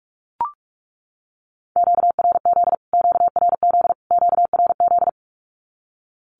Morse code, heard as a keyed tone, about 44 wpm, 700 Hz. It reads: QRZ QRZ QRZ
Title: Q-Codes Repeated 3x 45wpm